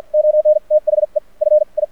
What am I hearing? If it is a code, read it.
Morse code, heard as a keyed tone, about 54 wpm, spaced as for 45 wpm, 610 Hz. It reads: ON THE FI
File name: archived / CRSnet